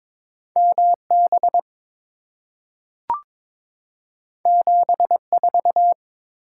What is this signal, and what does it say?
Morse code, heard as a keyed tone, about 22 wpm, 700 Hz. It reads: MB 74
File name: practice / 22wpm / Two Letter-Number Combinations - 22wpm